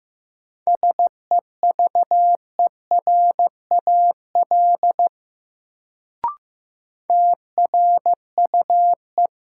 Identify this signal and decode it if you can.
Morse code, heard as a keyed tone, about 15 wpm, 700 Hz. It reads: SEVERAL TRUE
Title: Sentences from Top 400 Words - New Word Drill - 15wpm